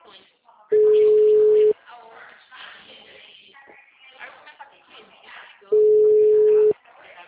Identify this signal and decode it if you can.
Ringback tone (Europe, 425 Hz), with speech.